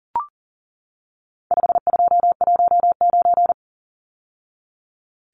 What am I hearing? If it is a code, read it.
Morse code, heard as a keyed tone, about 40 wpm, 700 Hz. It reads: 5219
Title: Numbers - 4-Digit 40wpm